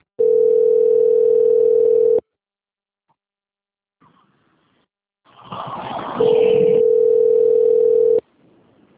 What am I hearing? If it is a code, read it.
Ringback tone (North America), 440 + 480 Hz.
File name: out- - - - - .55320